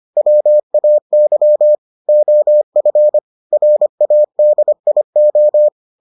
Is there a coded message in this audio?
Yes: Morse code from 0.2 to 5.7 s, heard as a keyed tone, about 25 wpm, 600 Hz. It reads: WAY OF RADIO